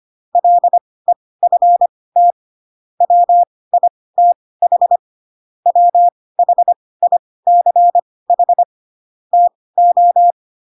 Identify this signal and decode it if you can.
Morse code, heard as a keyed tone, about 25 wpm, spaced as for 18 wpm, 700 Hz. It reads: LEFT WITH WHICH TO
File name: Pinocchio / Pinocchio ch6 12 25 ik1hge